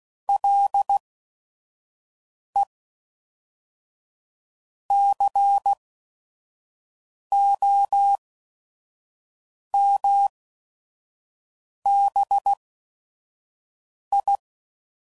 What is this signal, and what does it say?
Morse code, heard as a keyed tone, about 16 wpm, spaced as for 5 wpm, 800 Hz. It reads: LE COMBI
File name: CW Code Groups @5WPM / Technical Subjects / AND Gate